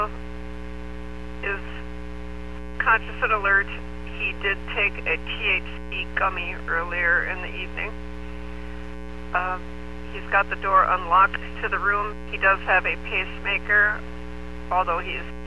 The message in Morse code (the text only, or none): none